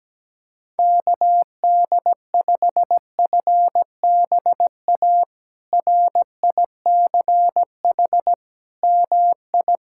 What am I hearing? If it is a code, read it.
Morse code, heard as a keyed tone, about 17 wpm, 700 Hz. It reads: KD5FBA RICH MI
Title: K1USN SST QSO Practice - 17wpm